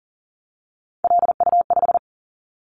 Morse code, heard as a keyed tone, about 41 wpm, 700 Hz. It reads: LU5
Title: Callsign Prefixes for Top 50 DXCC - 40wpm